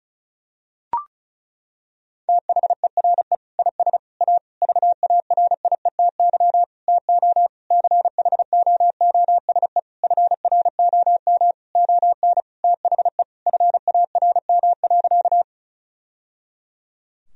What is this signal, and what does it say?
Morse code, heard as a keyed tone, about 35 wpm, 700 Hz. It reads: THERE IS A VARIETY TO CHOOSE FROM ON THE FARM.
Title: Sentences from Top 1000 Words - Review Part 4 - 35wpm